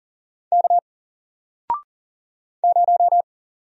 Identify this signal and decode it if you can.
Morse code, heard as a keyed tone, about 40 wpm, 700 Hz. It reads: K 0
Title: Single Letter-Number Alt 40wpm